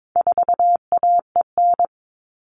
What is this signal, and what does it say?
Morse code, heard as a keyed tone, about 22 wpm, 700 Hz. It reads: 4AEN
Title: All Cut Numbers - 4-Digit 22wpm